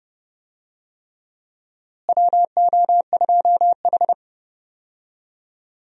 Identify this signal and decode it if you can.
Morse code, heard as a keyed tone, about 30 wpm, 700 Hz. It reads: WO2H
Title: USA Call Signs - 4 Characters - Repeated 3x - 30wpm